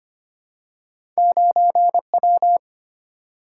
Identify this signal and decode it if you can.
Morse code, heard as a keyed tone, about 25 wpm, 700 Hz. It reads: 9W
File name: Two Letter-Number Combinations - 25wpm